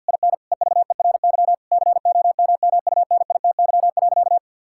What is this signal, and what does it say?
Morse code, heard as a keyed tone, about 50 wpm, 700 Hz. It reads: IN EVERY COMMUNITY.
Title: Sentences from Top 1000 Words - Review Part 5 - 50wpm